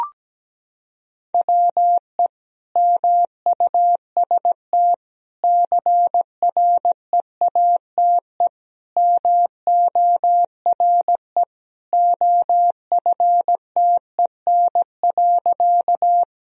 Morse code, heard as a keyed tone, about 17 wpm, 700 Hz. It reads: WE MUST CREATE MORE OFTEN.
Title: Sentences from Top 1000 Words - Review Part 3 - 17wpm